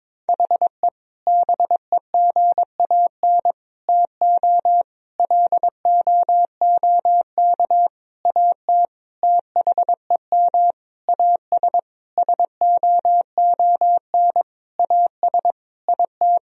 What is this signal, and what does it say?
Morse code, heard as a keyed tone, about 22 wpm, 700 Hz. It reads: HE BEGAN TO LOOK AT THEM AS SOON AS IT
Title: Sentences from Top 1000 Words - Review Part 1 - 22wpm